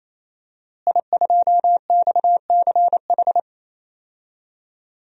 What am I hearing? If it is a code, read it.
Morse code, heard as a keyed tone, about 28 wpm, 700 Hz. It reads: I2XCH
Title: DXCC-100 Call Signs - 5 Characters - Repeated 3x - 28wpm